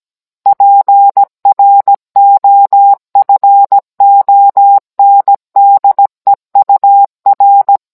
Morse code, heard as a keyed tone, about 17 wpm, 800 Hz. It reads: PROFONDEUR
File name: Cours-NR19-16mm